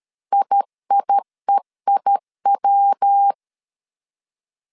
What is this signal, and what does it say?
Morse code, heard as a keyed tone, about 13 wpm, 800 Hz. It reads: IIEIW
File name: CW / DET / 1 3 1500C S C 12wpm 2750